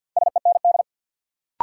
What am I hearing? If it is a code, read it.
Morse code, heard as a keyed tone, about 52 wpm, 700 Hz. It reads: SEND
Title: Sentences from Top 1500 Words - New Word Drill - 50wpm